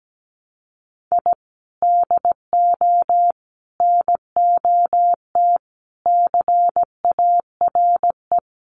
Morse code, heard as a keyed tone, about 17 wpm, 700 Hz. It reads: I DO NOT CARE
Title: Sets of 5 Words Encore2 17wpm